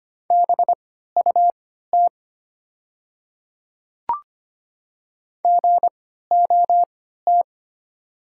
Morse code, heard as a keyed tone, about 25 wpm, spaced as for 14 wpm, 700 Hz. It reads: BUT GOT
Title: 3x Character Spacing - Top 300 Words - 25wpm